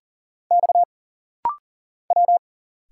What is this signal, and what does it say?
Morse code, heard as a keyed tone, about 40 wpm, 700 Hz. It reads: X W
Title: Single Letters - Rapid-Fire 40wpm